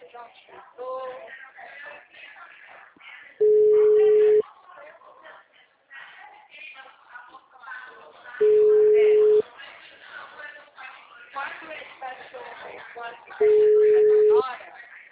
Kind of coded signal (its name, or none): telephone call progress tone